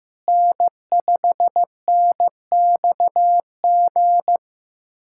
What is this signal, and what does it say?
Morse code, heard as a keyed tone, about 15 wpm, 700 Hz. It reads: N5NXG
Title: USA Callsign DE Callsign - 5 Characters - 15wpm